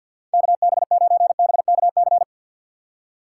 Morse code, heard as a keyed tone, about 47 wpm, 700 Hz. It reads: KB9BKC
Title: USA Call Signs - 6x Character Spacing - 50wpm